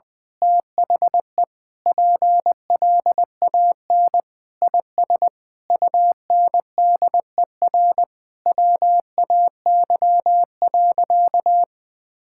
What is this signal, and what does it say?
Morse code, heard as a keyed tone, about 21 wpm, 700 Hz. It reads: THE PLAN IS UNDER WAY.
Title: Sentences with 9-Word Phrases - 20wpm